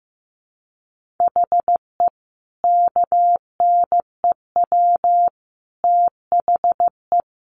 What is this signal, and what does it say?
Morse code, heard as a keyed tone, about 15 wpm, 700 Hz. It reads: HE KNEW THE
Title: Sentences from Top 1400 Words 15wpm